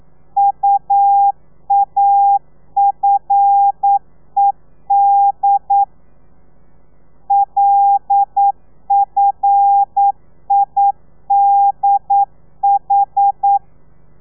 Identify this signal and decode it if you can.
Morse code, heard as a keyed tone, about 9 wpm, 800 Hz. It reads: UAFED LFIDH
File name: telegrafia / i2tfj / Lez5 40